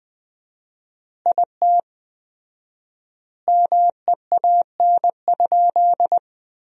Morse code, heard as a keyed tone, about 20 wpm, 700 Hz. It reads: IT MEAN?
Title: 4x Word-Spacing - Sentences from Top 200 Words - 20wpm